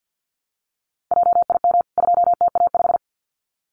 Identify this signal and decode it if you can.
Morse code, heard as a keyed tone, about 49 wpm, 700 Hz. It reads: JIM 2TA5
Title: CWT Exchange 50wpm